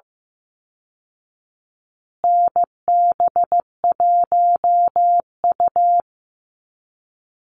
Morse code, heard as a keyed tone, about 15 wpm, 700 Hz. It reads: NB1U
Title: Call Signs Repeated 2x - 15wpm